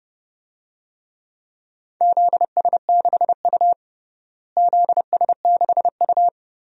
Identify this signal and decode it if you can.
Morse code, heard as a keyed tone, about 30 wpm, 700 Hz. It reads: ZS6U ZS6U
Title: DXCC-100 Call Signs - 4 Characters - Repeated 3x - 30wpm